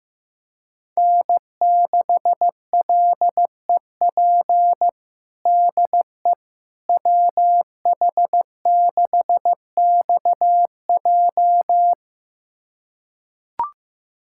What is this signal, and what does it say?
Morse code, heard as a keyed tone, about 15 wpm, 700 Hz. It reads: N6LEP DE WH6XJ
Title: USA Callsign DE Callsign - 5 Characters - 15wpm